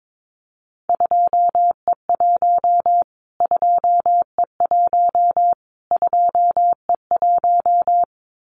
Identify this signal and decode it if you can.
Morse code, heard as a keyed tone, about 22 wpm, 700 Hz. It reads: 2E1 2E1 2E1
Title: Callsign Prefixes for Top 25 DXCC - Repeated 3x - 22wpm